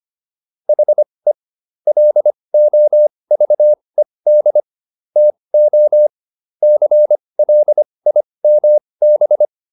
Morse code, heard as a keyed tone, about 25 wpm, spaced as for 20 wpm, 600 Hz. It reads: HE LOVED TO CLIMB